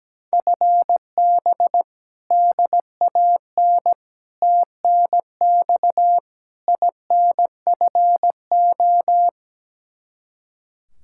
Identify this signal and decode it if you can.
Morse code, heard as a keyed tone, about 17 wpm, 700 Hz. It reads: FB DAN TNX INFO
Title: Practice QSO - Exchange 3 - More Info - 17wpm